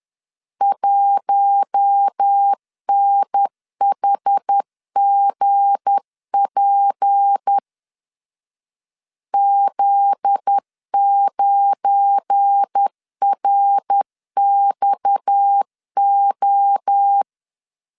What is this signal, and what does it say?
Morse code, heard as a keyed tone, about 10 wpm, 800 Hz. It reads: 1NHGP Z9RXO
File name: LEZIONI / 7 GF9Z / 7 GF9Z 10wpm 2611